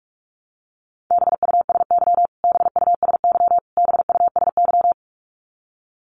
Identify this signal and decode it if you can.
Morse code, heard as a keyed tone, about 44 wpm, 700 Hz. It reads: BUSY BUSY BUSY
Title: Top 500 Words Repeated 45wpm